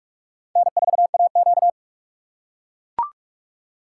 Morse code, heard as a keyed tone, about 44 wpm, 690 Hz. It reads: N4AQ
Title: USA Call Signs - 4 Characters - 2x Character Spacing - 45wpm